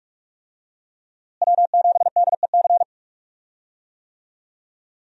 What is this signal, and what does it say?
Morse code, heard as a keyed tone, about 44 wpm, 700 Hz. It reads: W7DEC
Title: USA Call Signs - 5 Characters - Repeated 2x - 45wpm